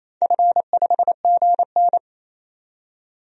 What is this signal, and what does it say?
Morse code, heard as a keyed tone, about 27 wpm, 700 Hz. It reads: F5GN
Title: DXCC-100 Call Signs - 4 Characters - 28wpm